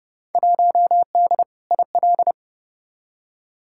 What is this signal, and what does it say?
Morse code, heard as a keyed tone, about 30 wpm, 700 Hz. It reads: 1D IL
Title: Field Day Exchanges - 30wpm